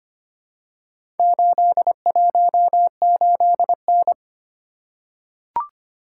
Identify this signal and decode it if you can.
Morse code, heard as a keyed tone, about 25 wpm, 700 Hz. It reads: 818N
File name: practice / 25wpm / Four Letter-Number Combinations - 25wpm